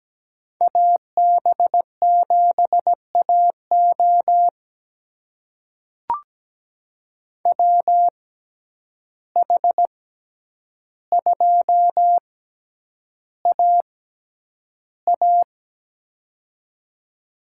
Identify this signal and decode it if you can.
Morse code, heard as a keyed tone, about 17 wpm, spaced as for 9 wpm, 700 Hz. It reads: AB7AO WH2AA
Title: USA Call Signs - 5 Characters - 6x Character Spacing - 17wpm